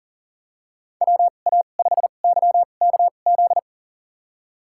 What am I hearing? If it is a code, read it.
Morse code, heard as a keyed tone, about 40 wpm, spaced as for 29 wpm, 700 Hz. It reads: WA5YKZ
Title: USA Call Signs - 6 Characters - 2x Character Spacing - 40wpm